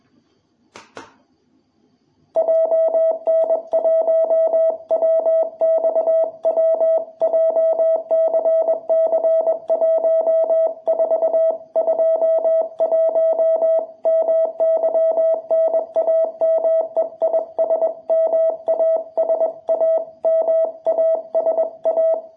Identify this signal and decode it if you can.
Morse code, heard as a keyed tone, about 21 wpm, 660 Hz. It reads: JN1WXW JCC1421 MY NAME IS MASA MASA
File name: jh8cbh / 2021cw